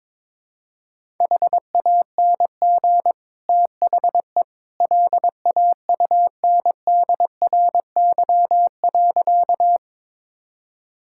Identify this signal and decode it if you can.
Morse code, heard as a keyed tone, about 22 wpm, 700 Hz. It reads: HANG THE LAUNDRY.
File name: practice / 22wpm / Easy Sentences with Hints - 22wpm